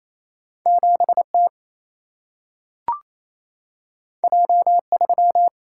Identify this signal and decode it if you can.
Morse code, heard as a keyed tone, about 28 wpm, 700 Hz. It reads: 7T J3
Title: Single-Letter Single-Number 28wpm